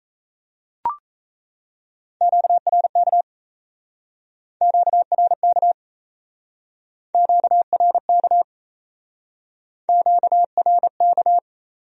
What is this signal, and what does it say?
Morse code, heard as a keyed tone, about 38 wpm, 700 Hz. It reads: QRK QRK QRK QRK